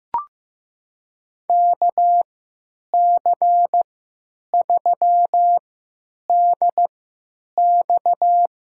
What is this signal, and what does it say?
Morse code, heard as a keyed tone, about 15 wpm, spaced as for 9 wpm, 700 Hz. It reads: KC3DX
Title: USA Call Signs - 5 Characters - 3x Character Spacing - 15wpm